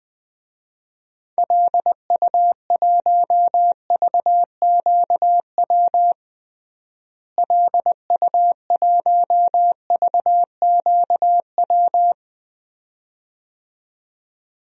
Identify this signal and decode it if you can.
Morse code, heard as a keyed tone, about 20 wpm, 700 Hz. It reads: LU1VQW LU1VQW
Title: DXCC-100 Call Signs - 6 Characters - Repeated 2x - 20wpm